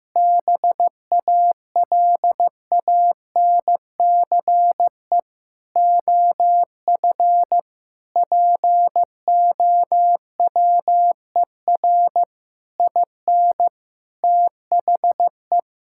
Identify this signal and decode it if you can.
Morse code, heard as a keyed tone, about 15 wpm, 700 Hz. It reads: BALANCE OF POWER IN THE
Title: Sentences from Top 1000 Words - Review Part 6 - 15wpm